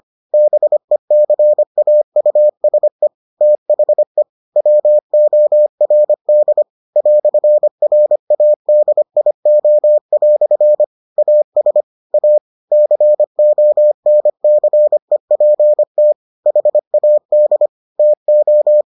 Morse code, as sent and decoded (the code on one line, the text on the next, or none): -... -.-. .- ..- ... / - .... / .-- --- .-. -.. / .-..-. .-. .- -.. .. --- .-..-. / .- ... / .- / -.-. --- -. -.-. .--. - / .... .- -.. / - ---
BECAUSE THE WORD "RADIO" AS A CONCEPT HAD TO